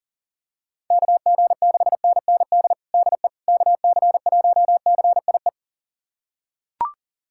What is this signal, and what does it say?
Morse code, heard as a keyed tone, about 37 wpm, 700 Hz. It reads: KG6NND DE KC1CIE
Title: USA Callsign DE Callsign - 6 Characters - 40wpm